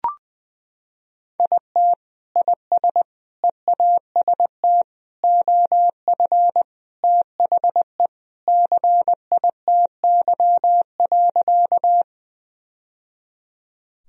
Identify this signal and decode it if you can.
Morse code, heard as a keyed tone, about 21 wpm, 700 Hz. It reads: IT IS EAST OF THE CITY.
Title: Sentences from Top 1000 Words - Review Part 3 - 20wpm